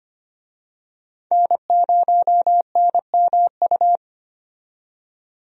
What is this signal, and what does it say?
Morse code, heard as a keyed tone, about 25 wpm, 700 Hz. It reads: N0NMU
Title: USA Call Signs - 5 Characters - 25wpm